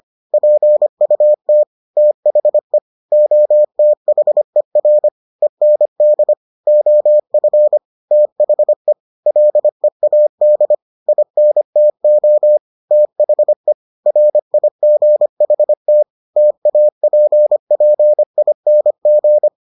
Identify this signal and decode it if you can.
Morse code, heard as a keyed tone, about 25 wpm, 600 Hz. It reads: PUT THE OTHER END OF THE LEAD INTO THE RIGHT TAPPING